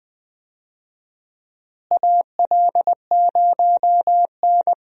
Morse code, heard as a keyed tone, about 20 wpm, 700 Hz. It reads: AL0N
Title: USA Call Signs - 4 Characters - Repeated 3x - 20wpm